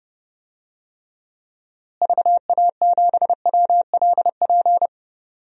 Morse code, heard as a keyed tone, about 31 wpm, 700 Hz. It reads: VA7WLP
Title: DXCC-100 Call Signs - 6 Characters - Repeated 2x - 30wpm